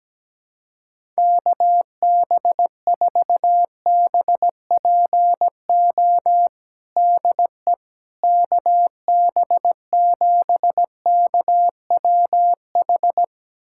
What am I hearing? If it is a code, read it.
Morse code, heard as a keyed tone, about 17 wpm, 700 Hz. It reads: KB4BPO DE KB7KWH
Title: USA Callsign DE Callsign - 6 Characters - 17wpm